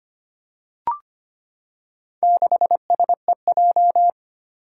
Morse code, heard as a keyed tone, about 25 wpm, 700 Hz. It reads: 6SEJ
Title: Four Challenging Letter-Number Combinations 25wpm